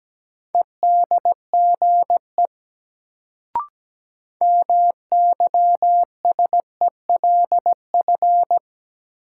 Morse code, heard as a keyed tone, about 17 wpm, 700 Hz. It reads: EDGE MYSELF
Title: Sentences from Top 1300 Words - New Word Drill Alt - 17wpm